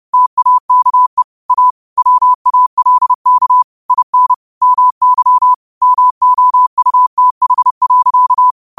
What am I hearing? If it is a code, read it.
Morse code, heard as a keyed tone, about 30 wpm, 1000 Hz. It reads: TAKE A WALK IN MY MOUTH.
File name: qotdfiles / 30wpm